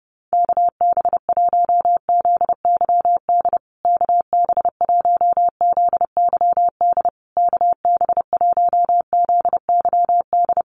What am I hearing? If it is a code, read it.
Morse code, heard as a keyed tone, about 30 wpm, 700 Hz. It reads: KB1ZYD KB1ZYD KB1ZYD
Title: USA Call Signs Repeated 3x 30wpm